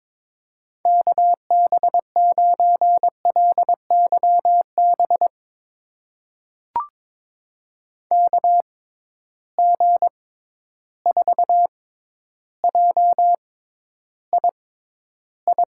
Morse code, heard as a keyed tone, about 22 wpm, spaced as for 11 wpm, 700 Hz. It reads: KB9LYB KG4JII
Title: USA Call Signs - 6 Characters - 6x Character Spacing - 22wpm